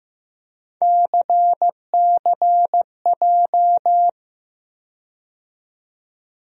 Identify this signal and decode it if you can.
Morse code, heard as a keyed tone, about 15 wpm, 700 Hz. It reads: CCJ
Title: USA Call Sign Postfixes - 3 Letters 15wpm